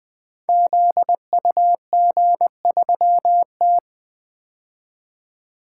Morse code, heard as a keyed tone, about 20 wpm, 700 Hz. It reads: ZUG3T
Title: Code Groups - 5 Letter:Numbers - 20wpm